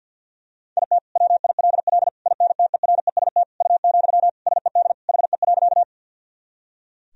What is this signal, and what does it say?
Morse code, heard as a keyed tone, about 50 wpm, 700 Hz. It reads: IT WILL INTEREST U, SED HE.
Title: Sentences from Top 400 Words and Abbreviations 50wpm